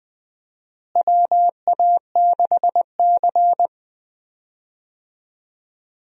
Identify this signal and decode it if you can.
Morse code, heard as a keyed tone, about 20 wpm, 700 Hz. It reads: WA6C
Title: USA Call Signs - 4 Characters - Repeated 3x - 20wpm